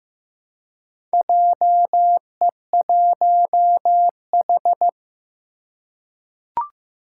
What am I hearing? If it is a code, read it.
Morse code, heard as a keyed tone, about 15 wpm, 700 Hz. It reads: JE1H
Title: Four Challenging Letter-Number Combinations 15wpm